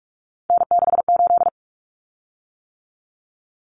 Morse code, heard as a keyed tone, about 44 wpm, 700 Hz. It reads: N68
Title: Common Cut Numbers - 3-Digit 45wpm